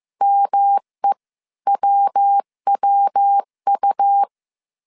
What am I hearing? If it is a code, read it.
Morse code, heard as a keyed tone, about 15 wpm, spaced as for 13 wpm, 800 Hz. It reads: MEWWU
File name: CW / DET / 1 3 1500C S A 14wpm 2447